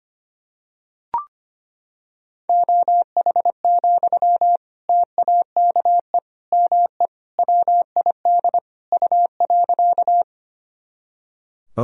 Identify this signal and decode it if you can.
Morse code, heard as a keyed tone, about 25 wpm, 700 Hz. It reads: OH, TAKE ME WID U.
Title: Sentences from Top 500 Words and Abbreviations 25wpm